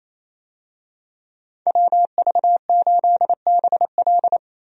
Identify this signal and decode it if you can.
Morse code, heard as a keyed tone, about 27 wpm, 700 Hz. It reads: WV8BL
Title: USA Call Signs - 5 Characters - 28wpm